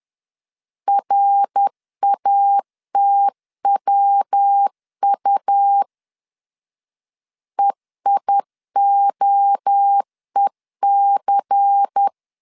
Morse code, heard as a keyed tone, about 10 wpm, 800 Hz. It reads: RATWU EIOEC
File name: CW / DET / 1 6 1500C S B 10wpm 4137